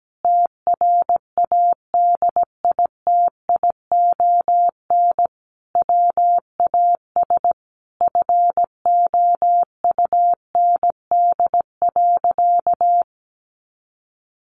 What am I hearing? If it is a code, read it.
Morse code, heard as a keyed tone, about 17 wpm, 700 Hz. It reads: TRADITION WAS FOUND.
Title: Sentences from Top 1500 Words 17wpm